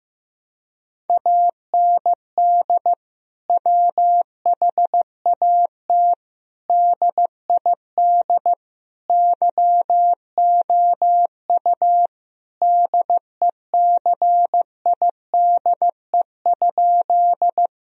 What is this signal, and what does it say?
Morse code, heard as a keyed tone, about 15 wpm, 700 Hz. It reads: AND WHAT DID YOU DECIDE?
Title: Sentences from Top 1000 Words - Review Part 3 - 15wpm